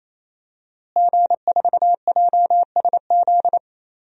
Morse code, heard as a keyed tone, about 28 wpm, 700 Hz. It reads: G4JSZ
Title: DXCC-100 Call Signs - 5 Characters - Repeated 3x - 28wpm